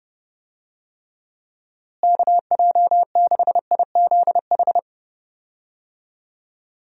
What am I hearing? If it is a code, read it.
Morse code, heard as a keyed tone, about 32 wpm, 700 Hz. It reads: KJ6IZH
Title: USA Call Signs - 6 Characters - Repeated 2x - 30wpm